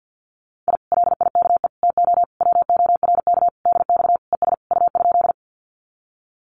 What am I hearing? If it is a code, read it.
Morse code, heard as a keyed tone, about 50 wpm, 700 Hz. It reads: I LIKE TO WORK DX ES U?
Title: Common QSO Phrases Encore 50wpm